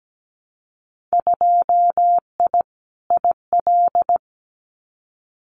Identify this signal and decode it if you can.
Morse code, heard as a keyed tone, about 17 wpm, 700 Hz. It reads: 2I IL
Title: Winter Field Day Exchanges 17wpm